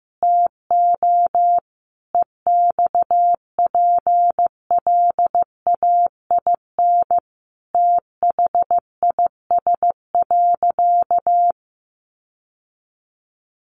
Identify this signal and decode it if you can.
Morse code, heard as a keyed tone, about 14 wpm, 700 Hz. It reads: TO EXPLAIN THIS.
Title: Sentences from Top 1100 Words 15wpm